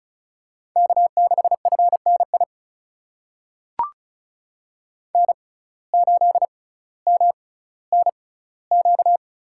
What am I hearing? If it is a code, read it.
Morse code, heard as a keyed tone, about 36 wpm, spaced as for 18 wpm, 700 Hz. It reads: K6FNI N8MNQ
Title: USA Call Signs - 5 Characters - 6x Character Spacing - 35wpm